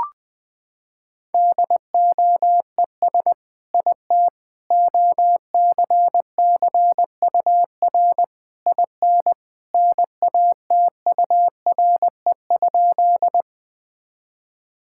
Morse code, heard as a keyed tone, about 20 wpm, 700 Hz. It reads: DOES IT OCCUR IN NATURE?
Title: Sentences from Top 1000 Words - Review Part 6 - 20wpm